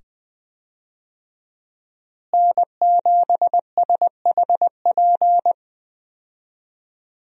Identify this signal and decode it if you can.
Morse code, heard as a keyed tone, about 20 wpm, 700 Hz. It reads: N7SHP
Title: USA Call Signs - 5 Characters - Repeated 2x - 20wpm